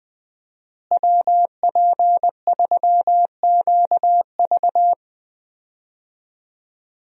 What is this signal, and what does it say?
Morse code, heard as a keyed tone, about 20 wpm, 700 Hz. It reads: WP3QV
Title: USA Call Signs - 5 Characters - Repeated 3x - 20wpm